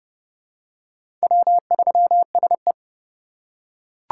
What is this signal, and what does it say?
Morse code, heard as a keyed tone, about 28 wpm, 700 Hz. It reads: W3SE
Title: USA Call Signs - 4 Characters - 3x Character Spacing - 30wpm